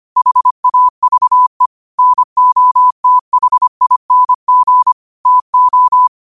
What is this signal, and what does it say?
Morse code, heard as a keyed tone, about 25 wpm, 1000 Hz. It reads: HAVE NOTHING TO
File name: qotdfiles / 25wpm